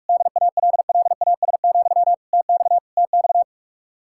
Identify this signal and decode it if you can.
Morse code, heard as a keyed tone, about 48 wpm, 700 Hz. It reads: DALLAS, TX TX